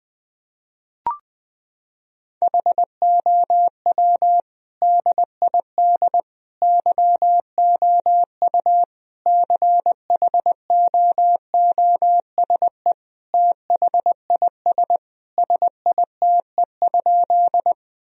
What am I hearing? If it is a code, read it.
Morse code, heard as a keyed tone, about 21 wpm, 700 Hz. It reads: HOW DID YOU CHOOSE THIS SITE?
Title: Sentences from Top 1000 Words - Review Part 4 - 20wpm